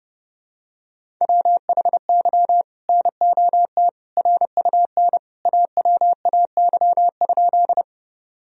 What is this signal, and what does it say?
Morse code, heard as a keyed tone, about 30 wpm, 700 Hz. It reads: WHY NOT RUN AWAY?
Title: Sentences from Top 1000 Words - Review Part 1 - 30wpm